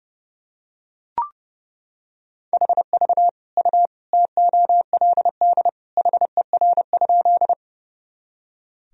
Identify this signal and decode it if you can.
Morse code, heard as a keyed tone, about 32 wpm, 700 Hz. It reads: HV U TOLD HER?
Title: Sentences from Top 400 Words and Abbreviations 30wpm